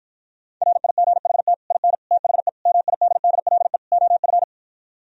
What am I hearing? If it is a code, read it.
Morse code, heard as a keyed tone, about 52 wpm, 700 Hz. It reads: RIGHT IN THE MIDDLE OF